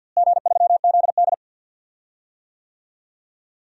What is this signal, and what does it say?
Morse code, heard as a keyed tone, about 49 wpm, 700 Hz. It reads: G3ZD
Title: DXCC-100 Call Signs - 4 Characters - Repeated 3x - 50wpm